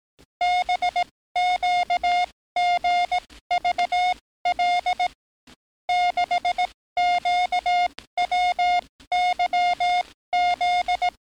Morse code, heard as a keyed tone, about 18 wpm, spaced as for 14 wpm, 710 Hz. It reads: BQGVL 6QWYZ
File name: training / NAVPERS 11049to11060 / NAVPERS 11057 RA RB